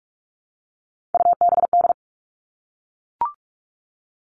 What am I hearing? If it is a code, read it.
Morse code, heard as a keyed tone, about 44 wpm, 700 Hz. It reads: UBD
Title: All Cut Numbers - 3-Digit 45wpm